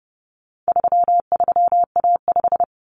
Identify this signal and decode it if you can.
Morse code, heard as a keyed tone, about 30 wpm, 700 Hz. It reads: 33A5
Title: Common Cut Numbers - 4-Digit 30wpm